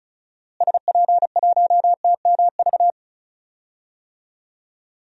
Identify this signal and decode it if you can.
Morse code, heard as a keyed tone, about 34 wpm, 700 Hz. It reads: SP1TMV
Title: DXCC-100 Call Signs - 6 Characters - Repeated 2x - 35wpm